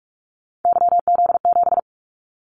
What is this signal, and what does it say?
Morse code, heard as a keyed tone, about 44 wpm, 700 Hz. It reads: YZ7
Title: Callsign Prefixes for Top 25 DXCC - 45wpm